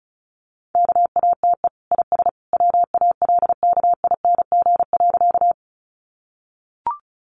Morse code, heard as a keyed tone, about 34 wpm, 700 Hz. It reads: KATE IS WALKING.
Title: Easy Phrases 35wpm